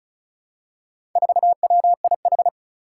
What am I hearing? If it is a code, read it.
Morse code, heard as a keyed tone, about 36 wpm, 700 Hz. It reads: 4WIH
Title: Four Challenging Letter-Number Combinations 35wpm